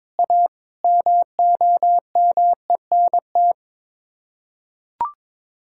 Morse code, heard as a keyed tone, about 22 wpm, 700 Hz. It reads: A MOMENT
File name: practice / 22wpm / Build A Phrase - 5-Word Phrases - 22wpm